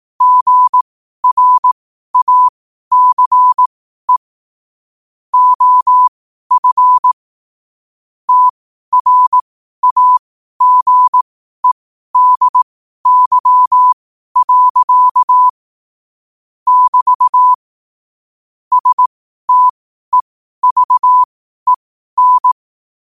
Morse code, heard as a keyed tone, about 18 wpm, spaced as for 12 wpm, 1000 Hz. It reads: GRACE OF TRAGEDY. <BT> STEVEN
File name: qotdfiles / 12wpm